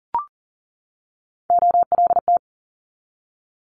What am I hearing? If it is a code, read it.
Morse code, heard as a keyed tone, about 38 wpm, 700 Hz. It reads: OLT
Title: Multi-letter Phonemes 40wpm